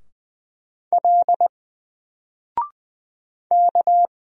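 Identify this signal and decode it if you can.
Morse code, heard as a keyed tone, about 20 wpm, 700 Hz. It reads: L K
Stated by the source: Single letter and number Morse drill at 20 wpm, from N4LCD.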